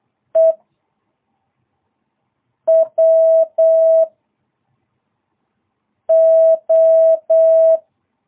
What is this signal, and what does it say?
Morse code, heard as a keyed tone, about 8 wpm, 660 Hz. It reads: E W O
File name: jh8cbh / 2022cw